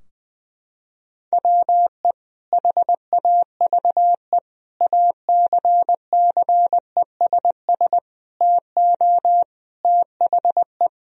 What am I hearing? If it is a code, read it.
Morse code, heard as a keyed tone, about 20 wpm, 700 Hz. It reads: WE HAVE ACCESS TO THE
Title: Sentences from Top 1000 Words - Review Part 4 - 20wpm